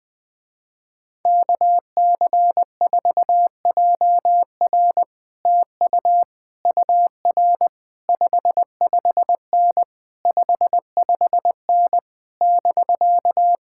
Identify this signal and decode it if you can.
Morse code, heard as a keyed tone, about 20 wpm, 700 Hz. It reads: KC4JR TU UR 55N 55N <BK>
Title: Speed-Racing Common QSO Phrases Encore - 17wpm